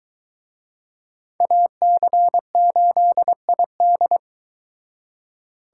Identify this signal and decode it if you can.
Morse code, heard as a keyed tone, about 24 wpm, 700 Hz. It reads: AC8ID